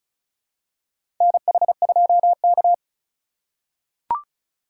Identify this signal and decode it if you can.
Morse code, heard as a keyed tone, about 34 wpm, 700 Hz. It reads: NH2K
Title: USA Call Signs - 4 Characters - 4x Character Spacing - 35wpm